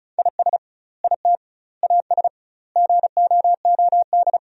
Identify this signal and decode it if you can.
Morse code, heard as a keyed tone, about 34 wpm, 700 Hz. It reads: IS IT AS GOOD
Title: 2x Word-Spacing - Sentences from Top 200 Words - 35wpm